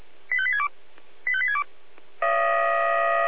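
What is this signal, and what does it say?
Five-tone selective call (ZVEI), heard as quick runs of five tones, tones of about 70 ms. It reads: 85862 85862
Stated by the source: ZVEI five-tone call received by SDR.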